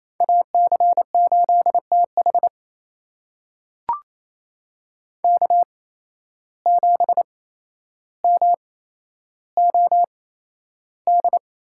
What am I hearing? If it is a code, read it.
Morse code, heard as a keyed tone, about 28 wpm, spaced as for 12 wpm, 700 Hz. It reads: AC8TH K7MOD
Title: USA Call Signs - 5 Characters - 8x Character Spacing - 28wpm